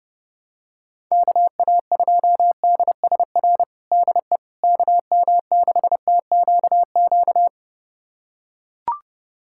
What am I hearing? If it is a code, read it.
Morse code, heard as a keyed tone, about 30 wpm, 700 Hz. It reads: KA2DSR DE KM6TQQ
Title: USA Callsign DE Callsign 30wpm